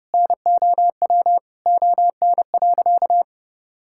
Morse code, heard as a keyed tone, about 30 wpm, 700 Hz. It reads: NOW ON.
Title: Sentences from Top 1000 Words - Review Part 6 - 30wpm